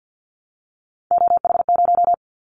Morse code, heard as a keyed tone, about 50 wpm, 700 Hz. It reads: OH0
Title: Callsign Prefixes for Top 100 DXCC - 50wpm